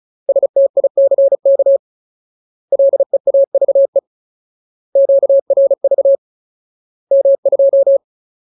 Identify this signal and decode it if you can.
Morse code, heard as a keyed tone, about 34 wpm, 550 Hz. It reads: STICK LEAVE QRV M2
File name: Ultimate Word List / 35WPM Ultimate